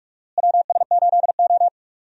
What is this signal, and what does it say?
Morse code, heard as a keyed tone, about 46 wpm, 700 Hz. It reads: WS8O
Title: USA Call Signs - 4 Characters - Repeated 2x - 45wpm